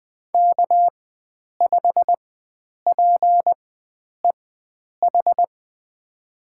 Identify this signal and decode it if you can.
Morse code, heard as a keyed tone, about 20 wpm, spaced as for 9 wpm, 700 Hz. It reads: K5PEH
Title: USA Call Signs - 4x Character Spacing - 20wpm